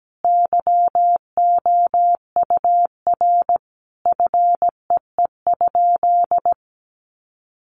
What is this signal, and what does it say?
Morse code, heard as a keyed tone, about 17 wpm, 700 Hz. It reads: YOUR FEE?
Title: Sentences from Top 1500 Words 17wpm